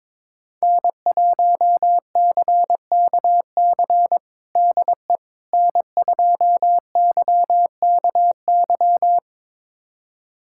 Morse code, heard as a keyed tone, about 22 wpm, 700 Hz. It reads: N1CKC DE N2YKY
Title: USA Callsign DE Callsign - 5 Characters - 22wpm